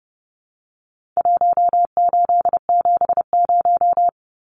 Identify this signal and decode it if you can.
Morse code, heard as a keyed tone, about 30 wpm, 700 Hz. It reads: 1870
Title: Numbers - Contest Style - 4-Digit 30wpm